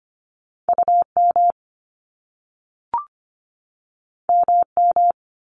Morse code, heard as a keyed tone, about 25 wpm, 700 Hz. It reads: UM MM
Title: Top Two Letter Combinations in Words 25wpm